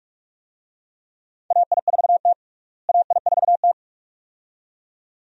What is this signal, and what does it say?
Morse code, heard as a keyed tone, about 46 wpm, 700 Hz. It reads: AI4T AI4T
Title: USA Call Signs - 4 Characters - Repeated 2x - 45wpm